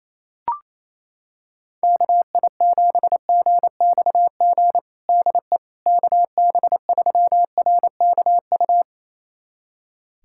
Morse code, heard as a keyed tone, about 28 wpm, 700 Hz. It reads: KI7GXG DE KB3RKU
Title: USA Callsign DE Callsign 28wpm